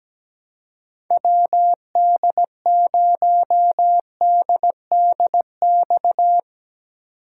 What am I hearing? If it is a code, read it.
Morse code, heard as a keyed tone, about 17 wpm, 700 Hz. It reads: WD0DDX
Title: USA Call Signs - 6 Characters - 2x Character Spacing - 17wpm